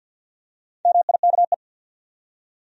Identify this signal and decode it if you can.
Morse code, heard as a keyed tone, about 52 wpm, 700 Hz. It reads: MIKE
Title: USA Common Names - Max 4 Letters - 50wpm